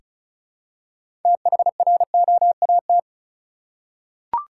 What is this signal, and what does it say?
Morse code, heard as a keyed tone, about 36 wpm, 700 Hz. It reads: THROAT
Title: Six Letter Words - 35wpm